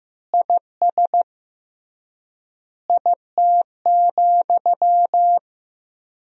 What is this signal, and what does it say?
Morse code, heard as a keyed tone, about 15 wpm, 700 Hz. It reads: IS IT,
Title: 3x Word-Spacing - Sentences from Top 200 Words - 15wpm